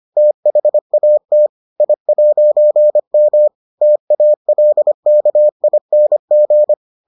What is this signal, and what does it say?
Morse code, heard as a keyed tone, about 25 wpm, 600 Hz. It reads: THAT I'M TALKING